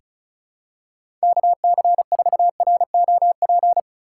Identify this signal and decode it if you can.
Morse code, heard as a keyed tone, about 36 wpm, 700 Hz. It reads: KC4ROP
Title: USA Call Signs - 6 Characters - 8x Character Spacing - 35wpm